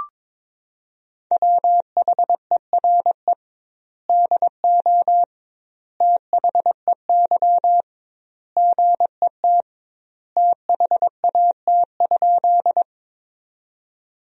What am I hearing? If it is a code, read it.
Morse code, heard as a keyed tone, about 22 wpm, 700 Hz. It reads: WHERE DO THEY GET THAT?
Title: 2x Word-Spacing - Sentences from Top 200 Words - 22wpm